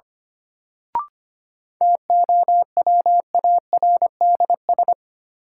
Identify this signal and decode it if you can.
Morse code, heard as a keyed tone, about 25 wpm, 700 Hz. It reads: TOWARDS
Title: Sentences from Top 1000 Words - Review Part 2 - New Word Drill - 25wpm